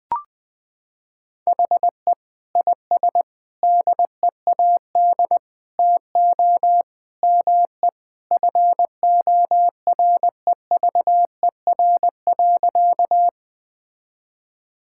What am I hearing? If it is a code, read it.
Morse code, heard as a keyed tone, about 20 wpm, 700 Hz. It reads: HE IS DEAD TO ME FOREVER.
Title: Sentences from Top 1600 Words 20wpm